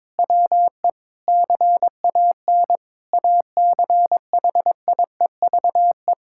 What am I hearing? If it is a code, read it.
Morse code, heard as a keyed tone, about 22 wpm, 700 Hz. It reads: WE CAN ACHIEVE
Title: Sentences from Top 1000 Words - Review Part 6 - 22wpm